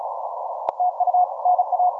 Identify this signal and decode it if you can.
Morse code, heard as a keyed tone, about 36 wpm, 720 Hz, with noise. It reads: TU NA
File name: simple-file-list / 6O6O 15MCW